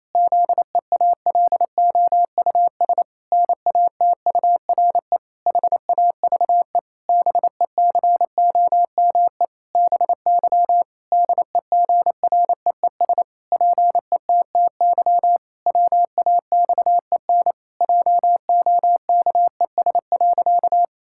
Morse code, heard as a keyed tone, about 28 wpm, 700 Hz. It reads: ZEALOUS NATURE HAVE BECOME BY DEGREES PETTY WAXEN JOKES.
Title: Pangram Challenge v2 - 28wpm